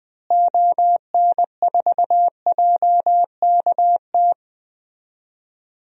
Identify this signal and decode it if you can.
Morse code, heard as a keyed tone, about 20 wpm, 700 Hz. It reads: ON4JKT
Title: DXCC-100 Call Signs - 6 Characters - Repeated 2x - 20wpm